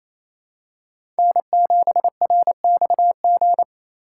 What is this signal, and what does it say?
Morse code, heard as a keyed tone, about 28 wpm, 700 Hz. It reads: N7RXG
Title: USA Call Signs - 5 Characters - 2x Character Spacing - 28wpm